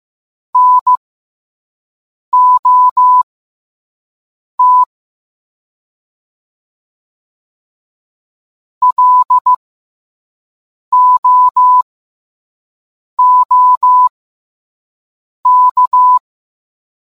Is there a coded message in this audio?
Yes: Morse code from 0.6 to 16.2 s, heard as a keyed tone, about 15 wpm, spaced as for 5 wpm, 1000 Hz. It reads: NOT LOOK